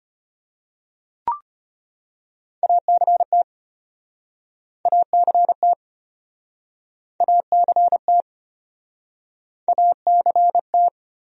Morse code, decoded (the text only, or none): ACT ACT ACT ACT